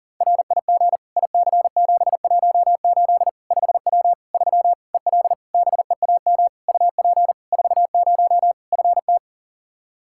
Morse code, heard as a keyed tone, about 37 wpm, 700 Hz. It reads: RIG IC718 5W 3 EL BEAM UP 40 FT
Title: K1USN SST QSO Practice 40wpm